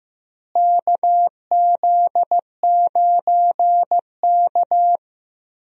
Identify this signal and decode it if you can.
Morse code, heard as a keyed tone, about 15 wpm, 700 Hz. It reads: KZ9K
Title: USA Call Signs - 4 Characters - Repeated 3x - 15wpm